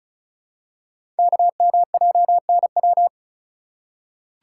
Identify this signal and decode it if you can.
Morse code, heard as a keyed tone, about 36 wpm, 700 Hz. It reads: KMJNW
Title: Code Groups - 5 Letters - 35wpm